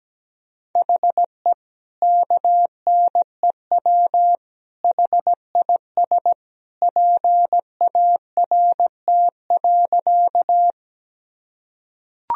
Morse code, heard as a keyed tone, about 17 wpm, 700 Hz. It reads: HE KNEW HIS PART.
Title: Sentences from Top 1000 Words - Review Part 1 - 17wpm